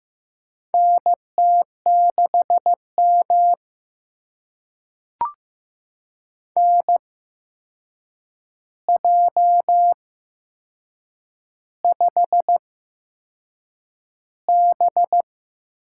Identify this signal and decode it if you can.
Morse code, heard as a keyed tone, about 15 wpm, spaced as for 6 wpm, 700 Hz. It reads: NT6M NJ5B
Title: USA Call Signs - 4 Characters - 8x Character Spacing - 15wpm